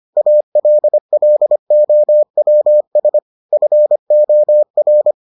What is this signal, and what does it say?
Morse code, heard as a keyed tone, about 25 wpm, 600 Hz. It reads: ALLOWS FOR